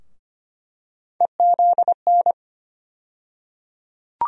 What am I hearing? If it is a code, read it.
Morse code, heard as a keyed tone, about 25 wpm, 700 Hz. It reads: EZN
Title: USA Call Sign Postfixes - 3 Letters 25wpm